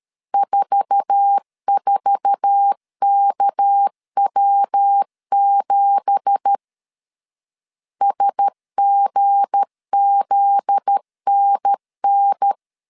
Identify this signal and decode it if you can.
Morse code, heard as a keyed tone, about 13 wpm, 800 Hz. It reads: 44KW7 SGZNN
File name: CW / DET / 1 9 1500C S C 12wpm 3811